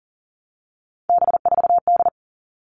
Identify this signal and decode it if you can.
Morse code, heard as a keyed tone, about 40 wpm, 700 Hz. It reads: B4D
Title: All Cut Numbers - 3-Digit 40wpm